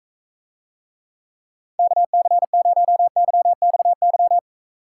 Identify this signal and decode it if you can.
Morse code, heard as a keyed tone, about 44 wpm, 700 Hz. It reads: KC0YXY